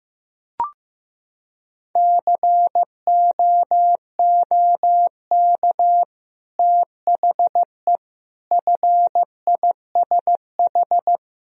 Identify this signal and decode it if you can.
Morse code, heard as a keyed tone, about 15 wpm, 700 Hz. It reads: COOK THE FISH
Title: Sentences from Top 1000 Words - Review Part 4 - 15wpm